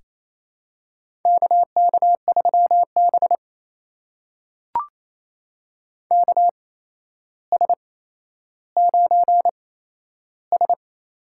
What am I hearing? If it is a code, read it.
Morse code, heard as a keyed tone, about 28 wpm, spaced as for 12 wpm, 700 Hz. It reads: KK3B KS9S
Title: USA Call Signs - 4 Characters - 8x Character Spacing - 28wpm